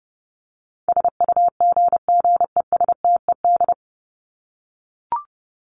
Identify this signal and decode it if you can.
Morse code, heard as a keyed tone, about 30 wpm, 700 Hz. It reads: SUGGESTED
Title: ED Suffix - Top 100 Words with Base Words 30wpm